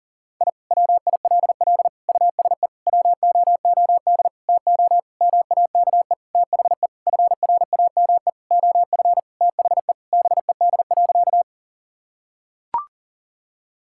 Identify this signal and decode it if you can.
Morse code, heard as a keyed tone, about 40 wpm, 700 Hz. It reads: I WILL USE WOOD TO MAKE THE FRAME OF THE BED.
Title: Sentences from Top 1000 Words - Review Part 6 - 40wpm